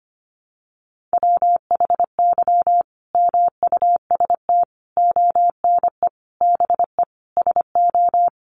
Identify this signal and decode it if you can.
Morse code, heard as a keyed tone, about 25 wpm, 700 Hz. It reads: WHY MUST ONE BE SO
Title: Sentences from Top 1400 Words 25wpm